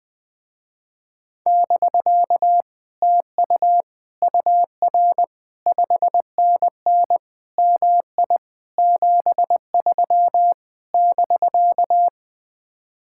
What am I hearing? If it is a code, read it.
Morse code, heard as a keyed tone, about 20 wpm, 700 Hz. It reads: <BK> TU UR 5NN MI 73 <BK>
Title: Speed-Racing Common QSO Phrases Encore - 17wpm